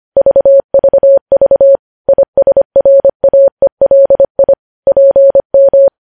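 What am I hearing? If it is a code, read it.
Morse code, heard as a keyed tone, about 25 wpm, 550 Hz. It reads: VVV ISRAELI PM
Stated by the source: Morse code news headlines sent at 25.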